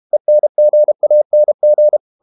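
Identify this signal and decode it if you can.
Morse code, heard as a keyed tone, about 32 wpm, 600 Hz. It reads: ENGANG
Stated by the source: High-speed QRQ CW at 32 WPM.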